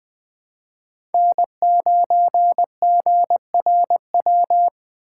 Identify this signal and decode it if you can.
Morse code, heard as a keyed tone, about 20 wpm, 700 Hz. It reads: N9GRW
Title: Speed Racing - USA 1x3 Call Signs 17wpm